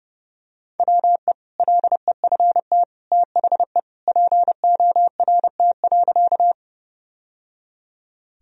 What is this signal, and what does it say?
Morse code, heard as a keyed tone, about 30 wpm, 700 Hz. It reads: WE LEFT THE PORT.
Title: Sentences from Top 1000 Words - Review Part 1 - 30wpm